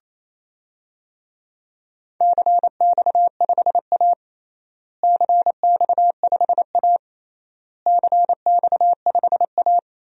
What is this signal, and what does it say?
Morse code, heard as a keyed tone, about 28 wpm, 700 Hz. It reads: CX5A CX5A CX5A
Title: DXCC-100 Call Signs - 4 Characters - Repeated 3x - 28wpm